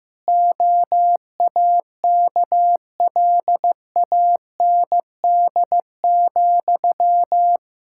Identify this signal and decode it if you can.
Morse code, heard as a keyed tone, about 16 wpm, 700 Hz. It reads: OAKLAND,
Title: K1USN SST QSO Practice 15wpm